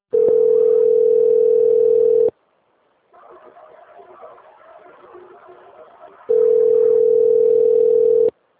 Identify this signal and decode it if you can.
Ringback tone (North America), 440 + 480 Hz.
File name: out- - - - - .51196